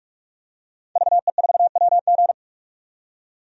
Morse code, heard as a keyed tone, about 46 wpm, 700 Hz. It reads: VE4WG
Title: DXCC-100 Call Signs - 5 Characters - 45wpm